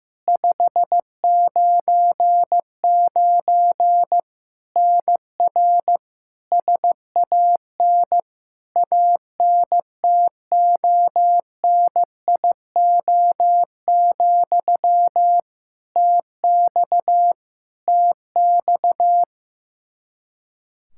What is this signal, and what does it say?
Morse code, heard as a keyed tone, about 15 wpm, 700 Hz. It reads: 599 NR SAN ANTONIO, TX TX
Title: K1USN SST QSO Practice 15wpm